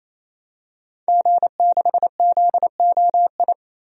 Morse code, heard as a keyed tone, about 28 wpm, 700 Hz. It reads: G6ZOI
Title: DXCC-100 Call Signs - 5 Characters - 28wpm